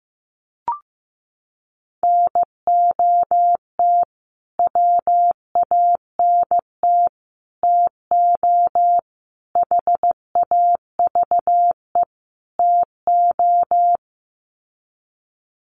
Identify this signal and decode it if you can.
Morse code, heard as a keyed tone, about 15 wpm, 700 Hz. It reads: NOT WANT TO HAVE TO
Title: Sets of 5 Words Encore2 15wpm